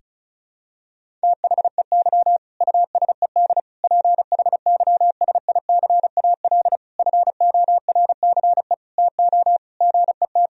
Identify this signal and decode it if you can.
Morse code, heard as a keyed tone, about 34 wpm, 700 Hz. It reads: THEY USED PHYSICAL FORCE TO GET
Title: Sentences from Top 1000 Words - Review Part 4 - 35wpm